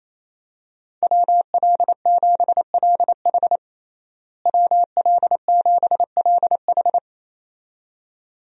Morse code, heard as a keyed tone, about 27 wpm, 700 Hz. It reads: WL7LH WL7LH
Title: USA Call Signs - 5 Characters - Repeated 2x - 28wpm